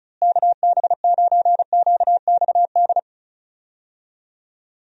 Morse code, heard as a keyed tone, about 34 wpm, 700 Hz. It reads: KB9QXD
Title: USA Call Signs - 6 Characters - Repeated 3x - 35wpm